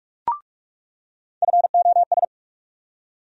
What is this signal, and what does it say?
Morse code, heard as a keyed tone, about 42 wpm, 700 Hz. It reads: FOS
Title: USA Call Sign Postfixes - 3 Letters 45wpm